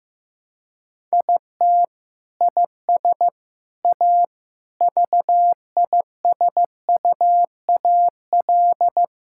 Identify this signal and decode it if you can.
Morse code, heard as a keyed tone, about 15 wpm, 700 Hz. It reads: IT IS A VISUAL
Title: Sentences from Top 1000 Words - Review Part 6 - 15wpm